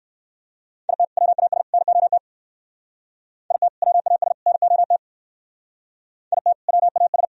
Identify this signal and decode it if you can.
Morse code, heard as a keyed tone, about 58 wpm, 700 Hz. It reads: IT WAS NOT IT WAS NOT IT WAS